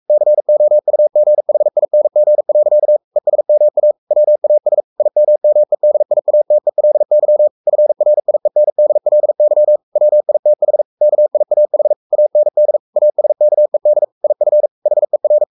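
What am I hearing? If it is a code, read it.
Morse code, heard as a keyed tone, about 44 wpm, 600 Hz. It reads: COUGHING. ESMA WAS IMMEDIATELY FRIENDLY WITH KIAH AND ASKED IF HER
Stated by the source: Morse code at 43 wpm sending Ken Follett's novel Never in English, from CW med Gnister.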